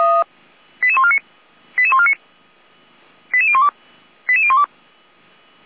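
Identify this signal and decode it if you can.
Five-tone selective call (ZVEI), heard as quick runs of five tones, tones of about 70 ms. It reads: 88149 88149 88012 88012
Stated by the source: ZVEI five-tone selective call received on SDR, Feuerwehr Orlamünde.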